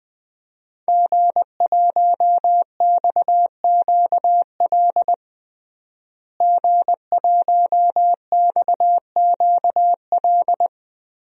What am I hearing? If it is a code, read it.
Morse code, heard as a keyed tone, about 20 wpm, 700 Hz. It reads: G1XQL G1XQL
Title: DXCC-100 Call Signs - 5 Characters - Repeated 3x - 20wpm